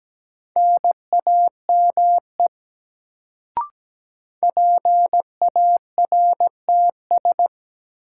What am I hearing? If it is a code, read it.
Morse code, heard as a keyed tone, about 17 wpm, 700 Hz. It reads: NAME PARTS
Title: Sets of 3 Words - New Word Drill - 17wpm